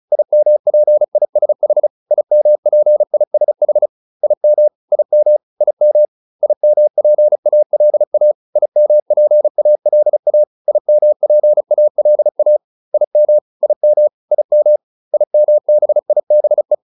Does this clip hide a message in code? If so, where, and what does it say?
Yes: Morse code from 0.1 to 16.7 s, heard as a keyed tone, about 34 wpm, 600 Hz. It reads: IMPISH IMPISH IM IM IM IMPALA IMPALA IMPALA IM IM IM IMBIBE